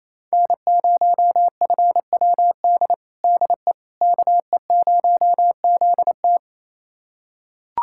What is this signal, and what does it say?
Morse code, heard as a keyed tone, about 28 wpm, 700 Hz. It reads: N0FWD DE KE0ZT
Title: USA Callsign DE Callsign - 5 Characters - 28wpm